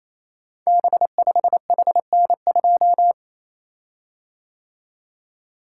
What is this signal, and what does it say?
Morse code, heard as a keyed tone, about 27 wpm, 700 Hz. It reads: B5HN2
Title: Code Groups - 5 Letter:Numbers - 28wpm